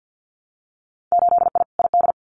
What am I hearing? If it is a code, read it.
Morse code, heard as a keyed tone, about 50 wpm, 700 Hz. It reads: 8I ID
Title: Winter Field Day Exchanges Encore 50wpm